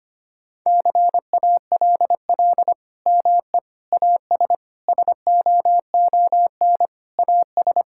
Morse code, heard as a keyed tone, about 25 wpm, 700 Hz. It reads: CALL ME AS SOON AS
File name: practice / 25wpm / Sentences with 3-Word Phrases - 25wpm